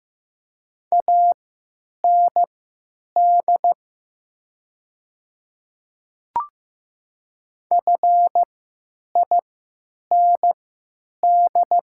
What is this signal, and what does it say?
Morse code, heard as a keyed tone, about 15 wpm, spaced as for 9 wpm, 700 Hz. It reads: AND FIND
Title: 3x Character Spacing - Top 200 Words - 15wpm